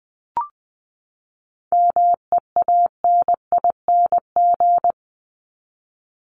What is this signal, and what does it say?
Morse code, heard as a keyed tone, about 20 wpm, 700 Hz. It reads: MEANING
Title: Top 751-1000 Words 20wpm